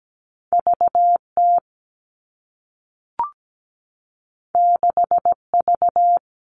Morse code, heard as a keyed tone, about 17 wpm, 700 Hz. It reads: VT 6V
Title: All Cut Numbers - 2-Digit 17wpm